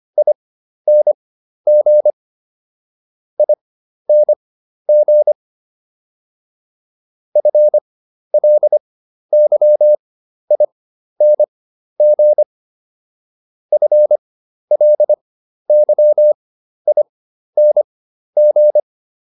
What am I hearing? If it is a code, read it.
Morse code, heard as a keyed tone, about 25 wpm, spaced as for 12 wpm, 600 Hz. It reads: ING ING FLYING FLYING